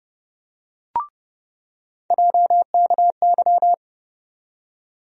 Morse code, heard as a keyed tone, about 30 wpm, 700 Hz. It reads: JKY
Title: USA Call Sign Postfixes - 3 Letters 30wpm